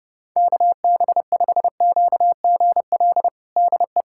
Morse code, heard as a keyed tone, about 30 wpm, 700 Hz. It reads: KB5QGL DE
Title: USA Callsign DE Callsign - 6 Characters - 30wpm